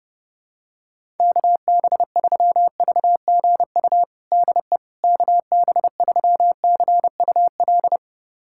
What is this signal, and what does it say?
Morse code, heard as a keyed tone, about 30 wpm, 700 Hz. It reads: KB3VGU DE KB3CUL
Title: USA Callsign DE Callsign 30wpm